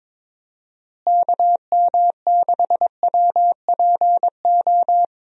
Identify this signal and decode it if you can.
Morse code, heard as a keyed tone, about 22 wpm, 700 Hz. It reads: KM6WPO
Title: USA Call Signs - 6 Characters - Repeated 2x - 22wpm